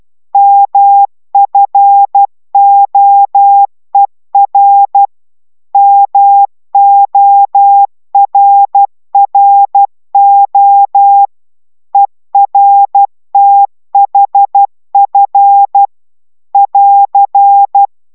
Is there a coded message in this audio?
Yes: Morse code from 0.3 to 17.9 s, heard as a keyed tone, about 12 wpm, 800 Hz. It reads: MFOER MORRO ERTHF <AR>